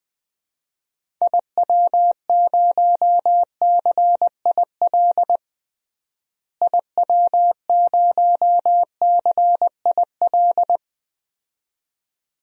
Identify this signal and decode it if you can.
Morse code, heard as a keyed tone, about 20 wpm, 700 Hz. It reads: IW0CIL IW0CIL
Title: DXCC-100 Call Signs - 6 Characters - Repeated 2x - 20wpm